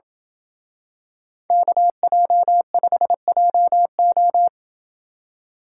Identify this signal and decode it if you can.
Morse code, heard as a keyed tone, about 27 wpm, 700 Hz. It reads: KJ5JO